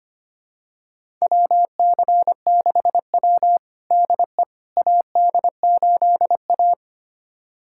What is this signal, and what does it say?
Morse code, heard as a keyed tone, about 25 wpm, 700 Hz. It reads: WC6W DE AD8A
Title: USA Callsign DE Callsign - 4 Characters - 25wpm